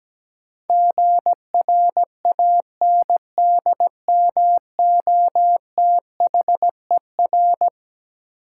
Morse code, heard as a keyed tone, about 17 wpm, 700 Hz. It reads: GRANDMOTHER
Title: Sentences from Top 1200 Words - New Word Drill Alt - 17wpm